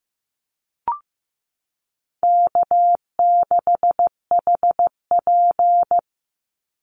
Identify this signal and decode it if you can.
Morse code, heard as a keyed tone, about 15 wpm, 700 Hz. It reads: K6HP
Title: Easy Call Signs Repeated 3x - 15wpm